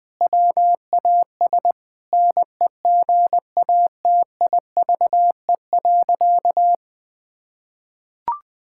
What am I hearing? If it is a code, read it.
Morse code, heard as a keyed tone, about 20 wpm, 700 Hz. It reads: WAS NEGATIVE.
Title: Sentences from Top 1000 Words - Review Part 6 - 20wpm